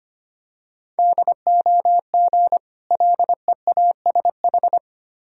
Morse code, heard as a keyed tone, about 25 wpm, 700 Hz. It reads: DOG LEASH
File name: practice / 25wpm / Difficult Words with Hints - 25wpm